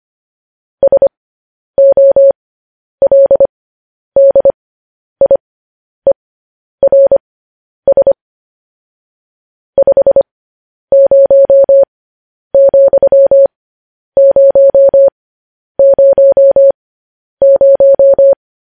Morse code, heard as a keyed tone, about 25 wpm, spaced as for 10 wpm, 550 Hz. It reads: SOLDIERS 50,000
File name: podcast / News headlines-1x-25x10